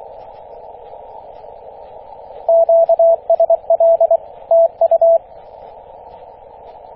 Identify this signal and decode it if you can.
Morse code, heard as a keyed tone, about 23 wpm, 660 Hz, with radio noise. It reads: QSL TU
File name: E6AC 21.030MHz CW